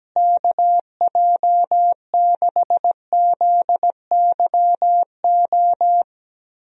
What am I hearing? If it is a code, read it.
Morse code, heard as a keyed tone, about 17 wpm, 700 Hz. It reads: KJ6ZYO
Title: USA Call Signs - 6 Characters - Repeated 3x - 17wpm